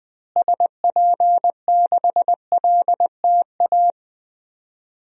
Morse code, heard as a keyed tone, about 20 wpm, 700 Hz. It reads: SP6LTA
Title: DXCC-100 Call Signs - 6 Characters - Repeated 3x - 20wpm